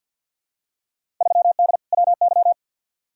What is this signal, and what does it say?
Morse code, heard as a keyed tone, about 51 wpm, 700 Hz. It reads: 3D WY
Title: Field Day Exchanges Encore - 50wpm